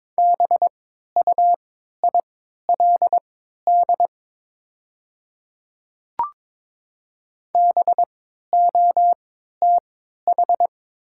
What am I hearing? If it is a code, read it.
Morse code, heard as a keyed tone, about 22 wpm, spaced as for 12 wpm, 690 Hz. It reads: BUILD BOTH
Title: 3x Character Spacing - Top 300 Words - 22wpm